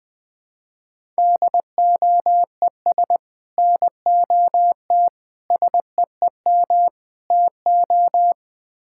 Morse code, heard as a keyed tone, about 20 wpm, 700 Hz. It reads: DOES NOT SEEM TO
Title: Build A Phrase - 5-Word Phrases - 20wpm